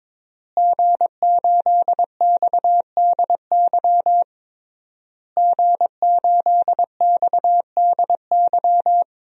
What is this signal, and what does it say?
Morse code, heard as a keyed tone, about 22 wpm, 700 Hz. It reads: G8XDY G8XDY
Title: DXCC-100 Call Signs - 5 Characters - Repeated 3x - 22wpm